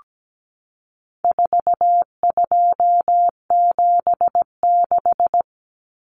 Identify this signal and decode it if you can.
Morse code, heard as a keyed tone, about 17 wpm, 700 Hz. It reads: 4276
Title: Common Cut Numbers - 4-Digit 17wpm